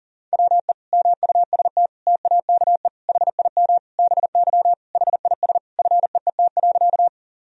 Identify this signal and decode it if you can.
Morse code, heard as a keyed tone, about 37 wpm, 700 Hz. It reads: WE MUST TAKE HIM BY HIS FEET.
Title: Sentences from Top 1000 Words - Review Part 1 - 40wpm